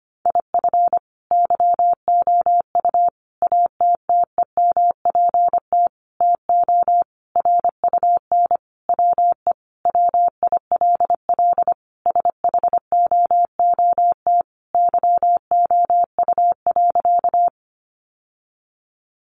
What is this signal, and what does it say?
Morse code, heard as a keyed tone, about 25 wpm, 700 Hz. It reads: IF YOU ATTEMPT TO RUN WE WILL SHOOT YOU.
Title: Sentences from Top 1200 Words 25wpm